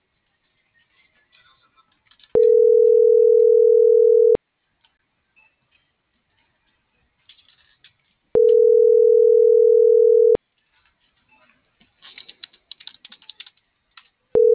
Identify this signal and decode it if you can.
Ringback tone (North America), 440 + 480 Hz.